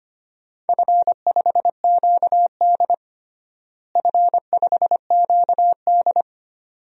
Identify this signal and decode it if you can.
Morse code, heard as a keyed tone, about 25 wpm, 700 Hz. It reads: F5QD F5QD
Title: DXCC-100 Call Signs - 4 Characters - Repeated 2x - 25wpm